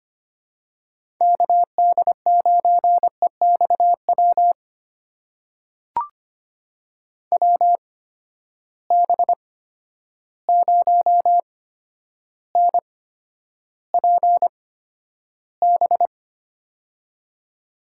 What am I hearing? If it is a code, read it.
Morse code, heard as a keyed tone, about 25 wpm, spaced as for 11 wpm, 700 Hz. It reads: KD9EXW WB0NPB
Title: USA Call Signs - 6 Characters - 8x Character Spacing - 25wpm